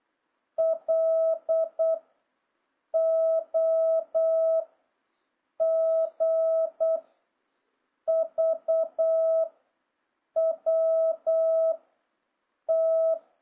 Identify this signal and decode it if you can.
Morse code, heard as a keyed tone, about 8 wpm, 650 Hz. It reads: L O G V W T
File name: jh8cbh / 2026cw